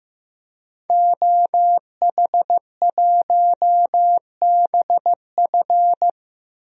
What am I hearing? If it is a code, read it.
Morse code, heard as a keyed tone, about 15 wpm, 700 Hz. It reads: OH1BF
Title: DXCC-100 Call Signs - 5 Characters - 15wpm